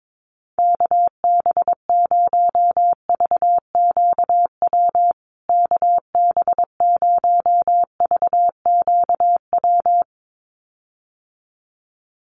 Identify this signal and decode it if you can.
Morse code, heard as a keyed tone, about 22 wpm, 700 Hz. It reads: KB0VQW KB0VQW
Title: USA Call Signs Repeated 2x 22wpm